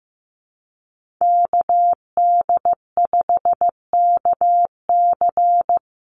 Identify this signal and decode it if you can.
Morse code, heard as a keyed tone, about 15 wpm, 700 Hz. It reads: KD5KC
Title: NA SOTA Chaser Callsigns 15wpm